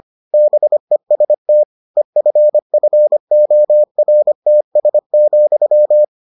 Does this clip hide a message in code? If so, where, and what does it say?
Yes: Morse code from 0.3 to 6.0 s, heard as a keyed tone, about 25 wpm, 600 Hz. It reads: BEST EFFORTS,